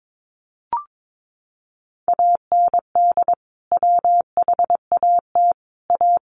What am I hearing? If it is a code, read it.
Morse code, heard as a keyed tone, about 22 wpm, 700 Hz. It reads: AND WHAT A
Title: Sentences from Top 500 Words 22wpm